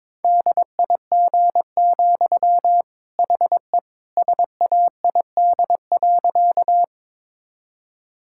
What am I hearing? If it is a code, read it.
Morse code, heard as a keyed tone, about 22 wpm, 700 Hz. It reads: DIG, HE SAID.
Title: Sentences from Top 1600 Words 22wpm